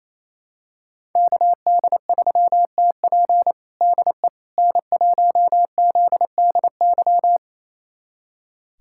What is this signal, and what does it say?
Morse code, heard as a keyed tone, about 27 wpm, 700 Hz. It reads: KD3TP DE N1ZDY
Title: USA Callsign DE Callsign - 5 Characters - 28wpm